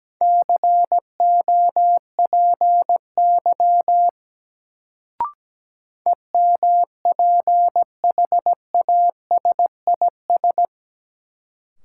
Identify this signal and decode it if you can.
Morse code, heard as a keyed tone, about 17 wpm, 700 Hz. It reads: COPY EMPHASIS
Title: Sentences from Top 1400 Words - New Word Drill - 17wpm